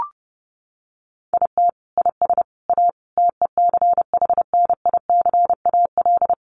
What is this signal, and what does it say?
Morse code, heard as a keyed tone, about 30 wpm, 700 Hz. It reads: IT IS A TECHNICAL
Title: Sentences from Top 1100 Words 30wpm